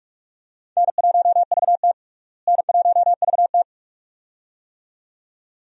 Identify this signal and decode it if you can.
Morse code, heard as a keyed tone, about 46 wpm, 700 Hz. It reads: N1VT N1VT
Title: USA Call Signs - 4 Characters - Repeated 2x - 45wpm